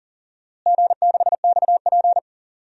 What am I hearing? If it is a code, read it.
Morse code, heard as a keyed tone, about 40 wpm, 700 Hz. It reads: G6XP
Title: DXCC-100 Call Signs - 4 Characters - Repeated 3x - 40wpm